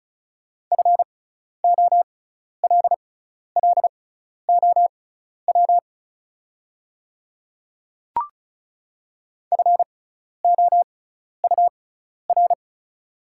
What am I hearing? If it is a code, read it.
Morse code, heard as a keyed tone, about 36 wpm, spaced as for 12 wpm, 690 Hz. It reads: FOLLOW FOUR
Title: 6x Character Spacing - Top 500 Words - 35wpm